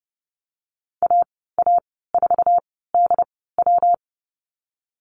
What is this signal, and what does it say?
Morse code, heard as a keyed tone, about 30 wpm, 700 Hz. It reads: A A 4 D W
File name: Lesson 19g - Letter w - Call Signs 30wpm